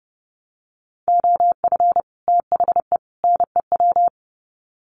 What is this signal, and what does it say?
Morse code, heard as a keyed tone, about 30 wpm, 700 Hz. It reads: OF THE NEW
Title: Sets of 3 Words 30wpm